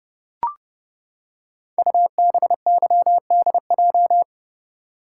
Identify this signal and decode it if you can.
Morse code, heard as a keyed tone, about 28 wpm, 700 Hz. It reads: UBYDJ
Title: Code Groups - 5 Letters - 30wpm